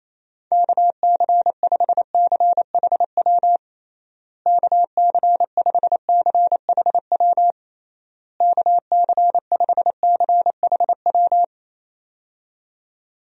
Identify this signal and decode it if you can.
Morse code, heard as a keyed tone, about 28 wpm, 700 Hz. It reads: KC5CHW KC5CHW KC5CHW
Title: USA Call Signs - 6 Characters - Repeated 3x - 28wpm